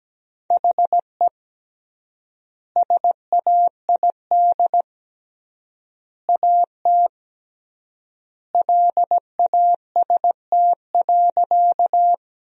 Morse code, heard as a keyed tone, about 17 wpm, 700 Hz. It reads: HE SAID AT LAST.
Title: 3x Word-Spacing - Sentences from Top 300 Words - 17wpm